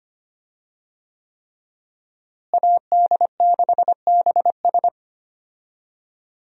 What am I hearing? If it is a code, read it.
Morse code, heard as a keyed tone, about 25 wpm, 700 Hz. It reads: AD6BS
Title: USA Call Signs - 5 Characters - Repeated 3x - 25wpm